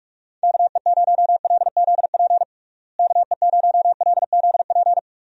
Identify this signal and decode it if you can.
Morse code, heard as a keyed tone, about 41 wpm, 700 Hz. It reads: KE0LZP KE0LZP
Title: USA Call Signs - 6 Characters - Repeated 3x - 45wpm